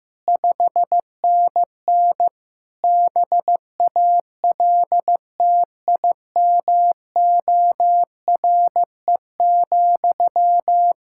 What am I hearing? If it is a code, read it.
Morse code, heard as a keyed tone, about 15 wpm, 700 Hz. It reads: 5NN BALTIMORE,
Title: Practice QSO - Exchange 1 - Beginning - 15wpm